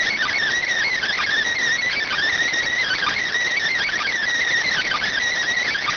Emergency Alert System (SAME) tones (header bursts), heard as harsh buzzing bursts. It is heard from 0.6 to 5.3 s.